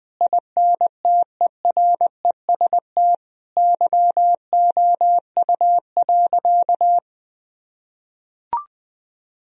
Morse code, heard as a keyed tone, about 20 wpm, 700 Hz. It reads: INTEREST YOU.
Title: Sentences from Top 1000 Words - Review Part 4 - 20wpm